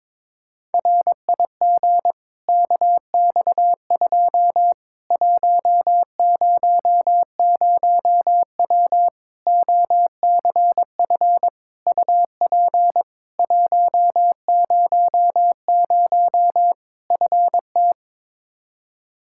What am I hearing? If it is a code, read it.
Morse code, heard as a keyed tone, about 22 wpm, 700 Hz. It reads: RIG KX2 100W OCF UP 100 FT
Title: Practice QSO - Exchange 2 - Info - 22wpm